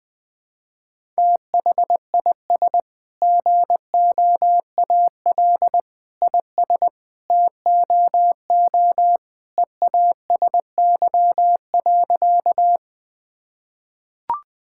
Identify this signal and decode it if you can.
Morse code, heard as a keyed tone, about 20 wpm, 700 Hz. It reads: THIS GOAL IS TOO EASY.
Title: Sentences from Top 1000 Words - Review Part 5 - 20wpm